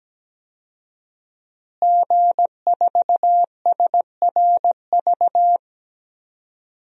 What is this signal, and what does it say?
Morse code, heard as a keyed tone, about 17 wpm, 700 Hz. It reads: G4SRV
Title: DXCC-100 Call Signs - 5 Characters - Repeated 3x - 17wpm